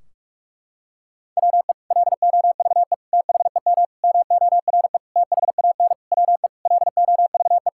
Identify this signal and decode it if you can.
Morse code, heard as a keyed tone, about 41 wpm, 700 Hz. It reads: WE LOVE THEM MORE THAN WE LOVE
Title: Sentences from Top 1600 Words 45wpm